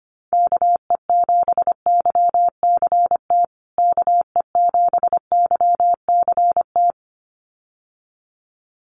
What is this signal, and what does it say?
Morse code, heard as a keyed tone, about 25 wpm, 700 Hz. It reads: KE7YCT KE7YCT
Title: USA Call Signs Repeated 2x 25wpm